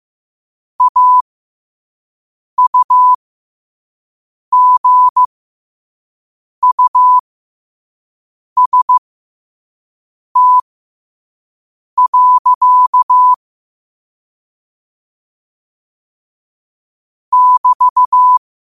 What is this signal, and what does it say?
Morse code, heard as a keyed tone, about 15 wpm, spaced as for 5 wpm, 1000 Hz. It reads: AUGUST. <BT>
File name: qotdfiles / 5wpm